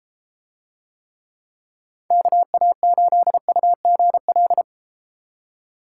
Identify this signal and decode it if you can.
Morse code, heard as a keyed tone, about 33 wpm, 700 Hz. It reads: KA8UGL